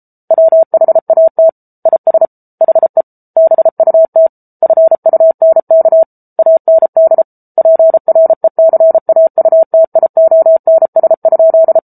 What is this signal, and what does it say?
Morse code, heard as a keyed tone, about 34 wpm, 650 Hz. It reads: WHAT IS HE BUT FUNK AND PRECAUTIONS?